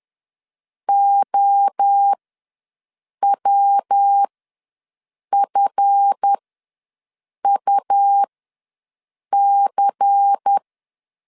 Morse code, heard as a keyed tone, about 10 wpm, spaced as for 6 wpm, 800 Hz. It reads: OWFUC
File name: LEZIONI / 7 GF9Z / 7 GF ws 10wpm 3918